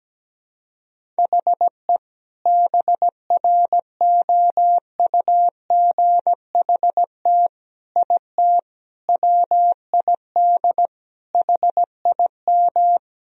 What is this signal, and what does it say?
Morse code, heard as a keyed tone, about 17 wpm, 700 Hz. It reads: HE BROUGHT IT WID HIM
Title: Sentences from Top 500 Words and Abbreviations 17wpm